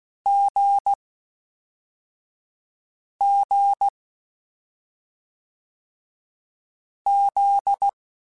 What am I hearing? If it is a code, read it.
Morse code, heard as a keyed tone, about 16 wpm, 800 Hz. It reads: G G Z
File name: Difficult Combos / Difficult Combos-mp3s / awj1 ndb6 gz7 @5